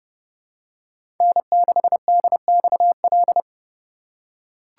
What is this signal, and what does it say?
Morse code, heard as a keyed tone, about 32 wpm, 700 Hz. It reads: N6DXL
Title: USA Call Signs - 5 Characters - 4x Character Spacing - 30wpm